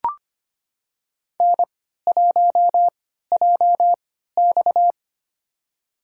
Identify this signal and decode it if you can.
Morse code, heard as a keyed tone, about 25 wpm, spaced as for 14 wpm, 700 Hz. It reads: N1JX
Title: USA Call Signs - 4 Characters - 3x Character Spacing - 25wpm